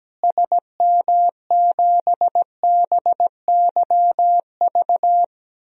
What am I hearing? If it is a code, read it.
Morse code, heard as a keyed tone, about 17 wpm, 700 Hz. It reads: SM7BYV
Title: DXCC-100 Call Signs - 6 Characters - Repeated 3x - 17wpm